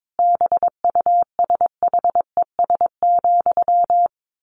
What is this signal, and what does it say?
Morse code, heard as a keyed tone, about 22 wpm, 700 Hz. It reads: BUSHES,
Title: A Sound Of Thunder 22wpm